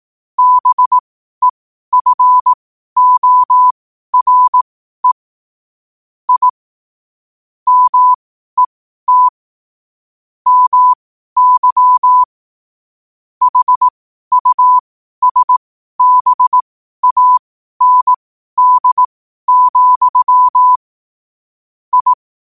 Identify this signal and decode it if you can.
Morse code, heard as a keyed tone, about 18 wpm, spaced as for 12 wpm, 1000 Hz. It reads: BEFORE I MET MY HUSBAND, I
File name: qotdfiles / 12wpm